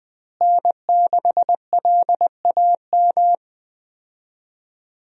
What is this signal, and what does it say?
Morse code, heard as a keyed tone, about 20 wpm, 700 Hz. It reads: N6LAM
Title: USA Call Signs - 5 Characters - Repeated 3x - 20wpm